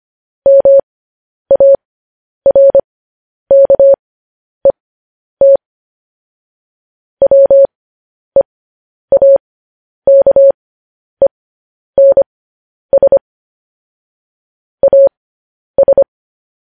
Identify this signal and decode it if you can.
Morse code, heard as a keyed tone, about 25 wpm, spaced as for 10 wpm, 550 Hz. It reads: MARKET WEAKENS AS
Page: News Headlines in Morse Code at 10 WPM